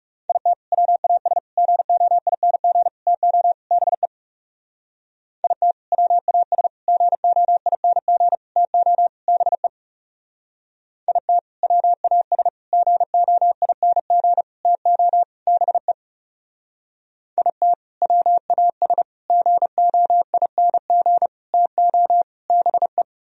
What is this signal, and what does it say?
Morse code, heard as a keyed tone, about 35 wpm, 700 Hz. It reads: IT WAS GOING TO BE IT WAS GOING TO BE IT WAS GOING TO BE IT WAS GOING TO BE
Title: Speed-Racing - Sets of 5 Words 30wpm